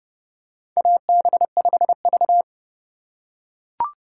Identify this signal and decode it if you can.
Morse code, heard as a keyed tone, about 32 wpm, 700 Hz. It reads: AB5V
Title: USA Call Signs - 4 Characters - 4x Character Spacing - 30wpm